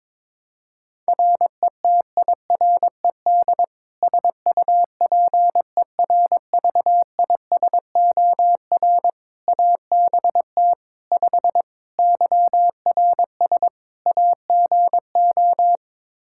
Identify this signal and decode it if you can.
Morse code, heard as a keyed tone, about 22 wpm, 700 Hz. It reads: RETIRED SUPERVISOR ABT 5 YRS AGO
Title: Practice QSO - Exchange 3 - More Info - 22wpm